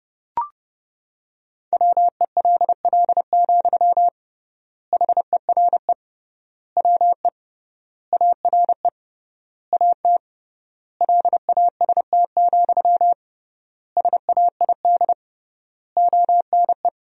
Morse code, heard as a keyed tone, about 32 wpm, 700 Hz. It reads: WELL, HERE WE ARE AT LAST, SAID ONE
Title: 3x Word-Spacing - Sentences from Top 300 Words - 30wpm